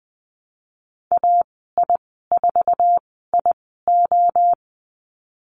Morse code, heard as a keyed tone, about 20 wpm, spaced as for 15 wpm, 700 Hz. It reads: AI4IO
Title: 128 Lesson 21g - Number 6 - Call Signs 20wpm